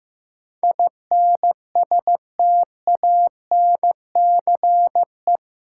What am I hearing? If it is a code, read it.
Morse code, heard as a keyed tone, about 15 wpm, 700 Hz. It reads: INSTANCE
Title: Sentences from Top 800 Words - New Word Drill - 15wpm